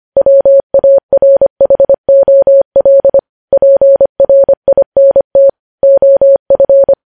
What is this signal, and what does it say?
Morse code, heard as a keyed tone, about 25 wpm, 550 Hz. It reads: WARHOL PRINT OF
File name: podcast / News headlines-1x-25x25